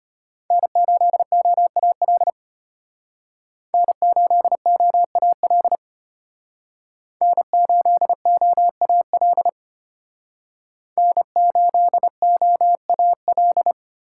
Morse code, heard as a keyed tone, about 33 wpm, 700 Hz. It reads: N8OAL N8OAL N8OAL N8OAL